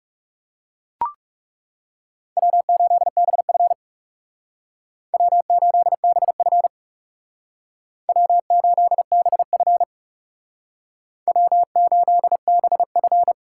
Morse code, heard as a keyed tone, about 36 wpm, 700 Hz. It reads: W8BF W8BF W8BF W8BF
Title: Speed Racing - USA 1x2 Call Signs 30wpm